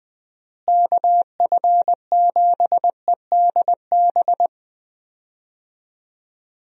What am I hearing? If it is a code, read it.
Morse code, heard as a keyed tone, about 20 wpm, 700 Hz. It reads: KF7EDB
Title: USA Call Signs - 6 Characters - Repeated 2x - 20wpm